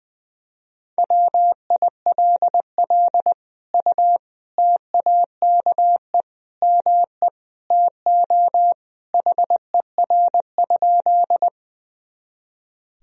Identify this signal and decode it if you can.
Morse code, heard as a keyed tone, about 20 wpm, 700 Hz. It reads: WILL U TAKE ME TO HER?
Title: Sentences from Top 200 Words and Abbreviations 20wpm